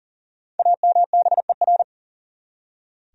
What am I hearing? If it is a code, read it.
Morse code, heard as a keyed tone, about 40 wpm, 700 Hz. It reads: AMBER
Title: USA Common Names - Max 5 Letters - 40wpm